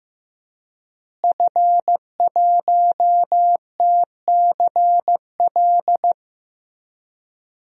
Morse code, heard as a keyed tone, about 15 wpm, 700 Hz. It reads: F1TCL
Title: DXCC-100 Call Signs - 5 Characters - Repeated 3x - 15wpm